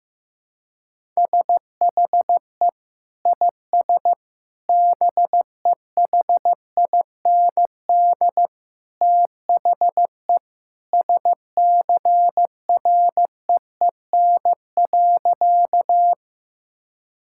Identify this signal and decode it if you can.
Morse code, heard as a keyed tone, about 15 wpm, 700 Hz. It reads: SHE IS BEHIND THE SCREEN.
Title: Sentences from Top 1000 Words - Review Part 6 - 15wpm